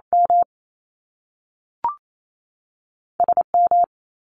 Morse code, heard as a keyed tone, about 28 wpm, 700 Hz. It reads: M S M
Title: Top Two Letter Combinations in Words 28wpm